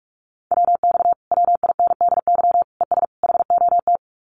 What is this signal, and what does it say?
Morse code, heard as a keyed tone, about 46 wpm, 700 Hz. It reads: WX WINDY ES HOT
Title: Common QSO Phrases Encore 45wpm